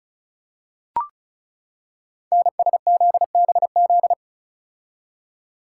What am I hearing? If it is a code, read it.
Morse code, heard as a keyed tone, about 36 wpm, 690 Hz. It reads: NSZBZ
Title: Code Groups - 5 Letters - 35wpm